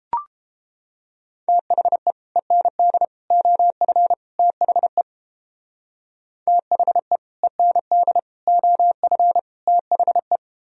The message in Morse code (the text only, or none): THE END OF THE THE END OF THE